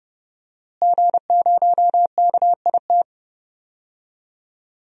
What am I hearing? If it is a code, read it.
Morse code, heard as a keyed tone, about 32 wpm, 700 Hz. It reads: G0KIT
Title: DXCC-100 Call Signs - 5 Characters - Repeated 2x - 30wpm